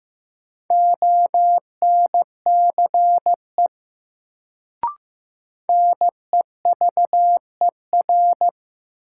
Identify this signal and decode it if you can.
Morse code, heard as a keyed tone, about 15 wpm, 700 Hz. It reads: ONCE NEVER
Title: Sentences from Top 300 Words - New Word Drill - 15wpm